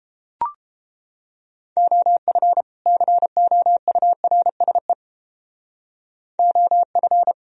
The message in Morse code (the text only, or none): OF COURSE OF